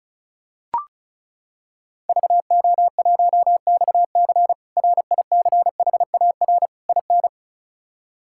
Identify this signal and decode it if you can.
Morse code, heard as a keyed tone, about 34 wpm, 700 Hz. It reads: VO1XC RICHAR IN
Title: K1USN SST QSO Practice - 35wpm